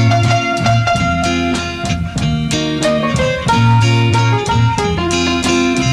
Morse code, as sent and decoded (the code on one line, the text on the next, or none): none